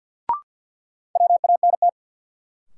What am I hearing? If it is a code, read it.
Morse code, heard as a keyed tone, about 52 wpm, 700 Hz. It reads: WANT
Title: Sets of 2 Words - New Word Drill - 50wpm